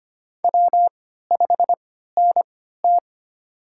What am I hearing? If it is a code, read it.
Morse code, heard as a keyed tone, about 25 wpm, spaced as for 14 wpm, 700 Hz. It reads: W5NT
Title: USA Call Signs - 4 Characters - 3x Character Spacing - 25wpm